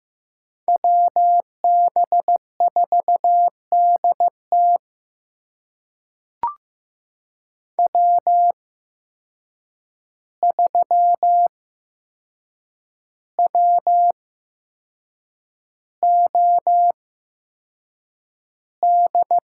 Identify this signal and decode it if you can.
Morse code, heard as a keyed tone, about 15 wpm, spaced as for 6 wpm, 700 Hz. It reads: WB4DT W3WOD
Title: USA Call Signs - 5 Characters - 8x Character Spacing - 15wpm